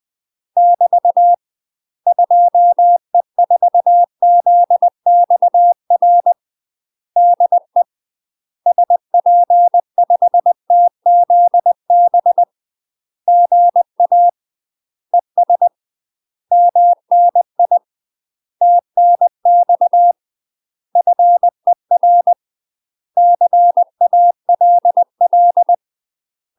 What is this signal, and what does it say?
Morse code, heard as a keyed tone, about 20 wpm, 700 Hz. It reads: <BT> 2E4ZXR DE SP5TZB GA ES MNI TNX FER CALL
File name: UK MASTER QSOs TRAINING / 20WPM / UK MASTER QSOs 0063